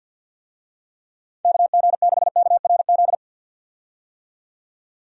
Morse code, heard as a keyed tone, about 50 wpm, 700 Hz. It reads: KG6KRZ
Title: USA Call Signs - 6 Characters - 50wpm